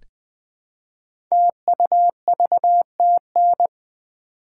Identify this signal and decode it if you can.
Morse code, heard as a keyed tone, about 20 wpm, 700 Hz. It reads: TUVTN
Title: Code Groups - 5 Letters - 20wpm